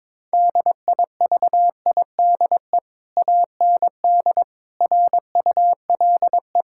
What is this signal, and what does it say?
Morse code, heard as a keyed tone, about 22 wpm, 700 Hz. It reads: DIVIDE AND RULE
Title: Sentences from Top 1600 Words 22wpm